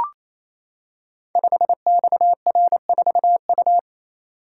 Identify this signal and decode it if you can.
Morse code, heard as a keyed tone, about 29 wpm, 700 Hz. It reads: 5XR4U
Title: Code Groups - 5 Letter:Numbers - 28wpm